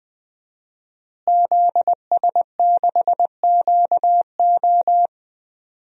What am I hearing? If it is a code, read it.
Morse code, heard as a keyed tone, about 20 wpm, 700 Hz. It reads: ZS6QO
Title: DXCC-100 Call Signs - 5 Characters - Repeated 3x - 20wpm